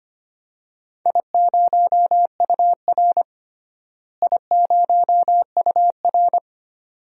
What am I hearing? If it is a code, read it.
Morse code, heard as a keyed tone, about 25 wpm, 700 Hz. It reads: I0UR I0UR
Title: DXCC-100 Call Signs - 4 Characters - Repeated 2x - 25wpm